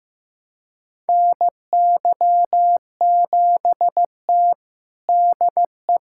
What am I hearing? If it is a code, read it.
Morse code, heard as a keyed tone, about 15 wpm, 700 Hz. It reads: NY7T DE
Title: USA Callsign DE Callsign - 4 Characters - 15wpm